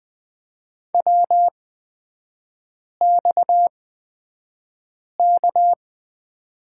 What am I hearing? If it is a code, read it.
Morse code, heard as a keyed tone, about 20 wpm, spaced as for 5 wpm, 700 Hz. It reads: WXK
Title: 243 Lesson 39e - Letter z - Three Characters 20wpm